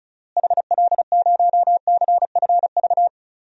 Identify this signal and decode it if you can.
Morse code, heard as a keyed tone, about 36 wpm, 700 Hz. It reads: HL0CFV
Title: DXCC-100 Call Signs - 6 Characters - Repeated 2x - 35wpm